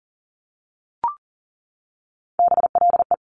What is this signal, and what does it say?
Morse code, heard as a keyed tone, about 42 wpm, 700 Hz. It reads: BLE
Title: Top Three Letter Combinations in Words 40wpm